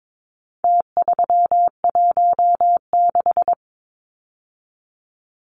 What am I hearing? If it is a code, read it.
Morse code, heard as a keyed tone, about 22 wpm, 700 Hz. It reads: T316
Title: Numbers - Contest Style - 4-Digit 22wpm